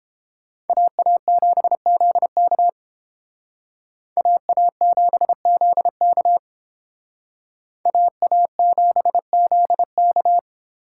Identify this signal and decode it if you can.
Morse code, heard as a keyed tone, about 30 wpm, 700 Hz. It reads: AA7ZK AA7ZK AA7ZK